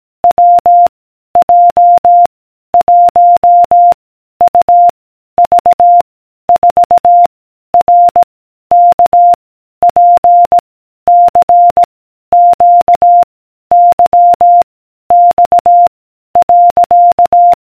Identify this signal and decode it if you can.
Morse code, heard as a keyed tone, about 18 wpm, 700 Hz. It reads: W J 1 U V 4 R K P C Q Y X .